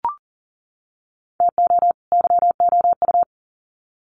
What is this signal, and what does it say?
Morse code, heard as a keyed tone, about 40 wpm, 700 Hz. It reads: TO YOU
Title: Sets of 2 Words 40wpm